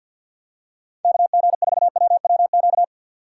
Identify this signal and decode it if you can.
Morse code, heard as a keyed tone, about 53 wpm, 700 Hz. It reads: KG4WWQ
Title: USA Call Signs - 6 Characters - 4x Character Spacing - 50wpm